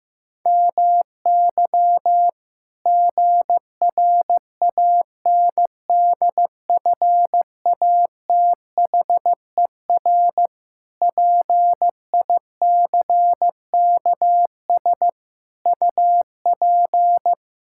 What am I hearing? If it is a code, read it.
Morse code, heard as a keyed tone, about 16 wpm, 700 Hz. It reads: MY GRANDFATHER PICKS UP
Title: Pangram Challenge v2 - 15wpm